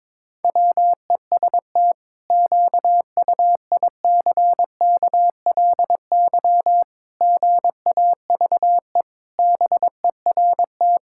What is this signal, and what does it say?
Morse code, heard as a keyed tone, about 22 wpm, 700 Hz. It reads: WEST QUICKLY GAVE BERT
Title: Pangram Challenge v2 - 22wpm